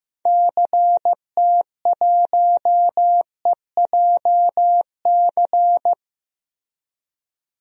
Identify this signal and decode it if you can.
Morse code, heard as a keyed tone, about 16 wpm, 700 Hz. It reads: CT1EJC
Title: DXCC-100 Call Signs - 6 Characters - Repeated 2x - 15wpm